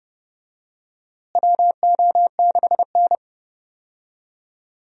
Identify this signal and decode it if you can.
Morse code, heard as a keyed tone, about 31 wpm, 700 Hz. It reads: WO6N
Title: USA Call Signs - 4 Characters - Repeated 2x - 30wpm